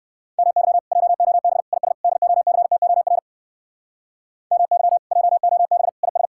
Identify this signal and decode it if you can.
Morse code, heard as a keyed tone, about 59 wpm, 700 Hz. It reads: MY JOB IS DOCTOR MY JOB IS